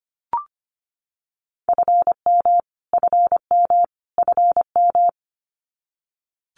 Morse code, heard as a keyed tone, about 25 wpm, 700 Hz. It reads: FM FM FM
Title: Abbreviations Repeated 3x - 25wpm